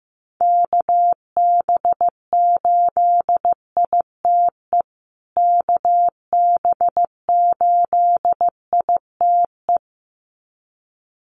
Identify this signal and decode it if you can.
Morse code, heard as a keyed tone, about 15 wpm, 700 Hz. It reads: KB8ITE KB8ITE
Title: USA Call Signs Repeated 2x 15wpm